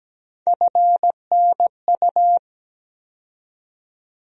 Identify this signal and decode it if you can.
Morse code, heard as a keyed tone, about 17 wpm, 700 Hz. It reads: FNU
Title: USA Call Sign Postfixes - 3 Letters 17wpm